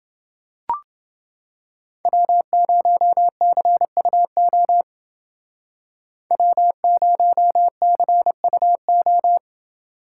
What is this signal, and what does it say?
Morse code, heard as a keyed tone, about 28 wpm, 700 Hz. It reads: W0CUO W0CUO